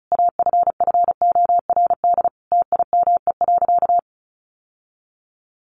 Morse code, heard as a keyed tone, about 34 wpm, 700 Hz. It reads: AFFORD TIME.
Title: Sentences from Top 1100 Words 35wpm